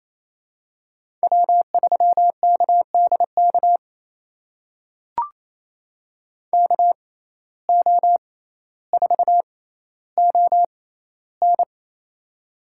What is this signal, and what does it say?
Morse code, heard as a keyed tone, about 28 wpm, spaced as for 14 wpm, 700 Hz. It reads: W3KDK KO4ON
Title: USA Call Signs - 5 Characters - 6x Character Spacing - 28wpm